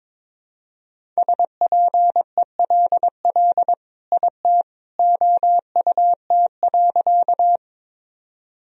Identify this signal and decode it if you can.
Morse code, heard as a keyed tone, about 22 wpm, 700 Hz. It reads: SPELL IT OUT.
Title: Sentences from Top 1000 Words - Review Part 1 - 22wpm